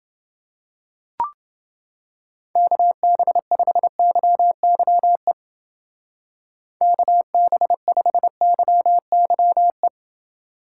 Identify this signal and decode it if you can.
Morse code, heard as a keyed tone, about 29 wpm, 700 Hz. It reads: KB5YYE KB5YYE